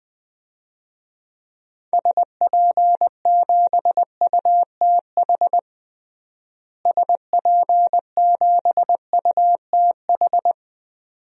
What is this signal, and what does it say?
Morse code, heard as a keyed tone, about 21 wpm, 700 Hz. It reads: SP7UTH SP7UTH
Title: DXCC-100 Call Signs - 6 Characters - Repeated 2x - 20wpm